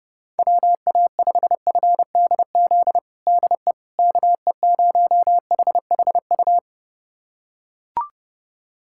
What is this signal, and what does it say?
Morse code, heard as a keyed tone, about 29 wpm, 700 Hz. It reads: WA5FDZ DE KE0HHU
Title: USA Callsign DE Callsign 30wpm